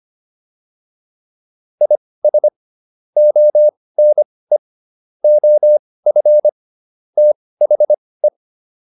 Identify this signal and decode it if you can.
Morse code, heard as a keyed tone, about 25 wpm, spaced as for 18 wpm, 600 Hz. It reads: IS ONE OF THE